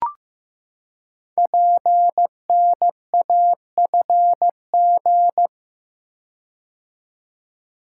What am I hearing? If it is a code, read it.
Morse code, heard as a keyed tone, about 15 wpm, 700 Hz. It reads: PNAFG
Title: Code Groups - 5 Letters - 15wpm